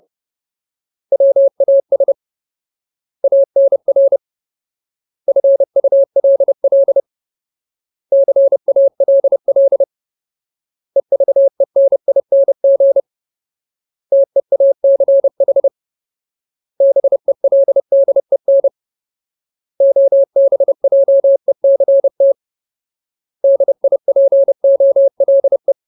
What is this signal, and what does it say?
Morse code, heard as a keyed tone, about 30 wpm, 550 Hz. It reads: WAS ANR FULL CALL EVENING TEACH BELDEN OBJECT DIPOLE
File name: Ultimate Word List / 30WPM Ultimate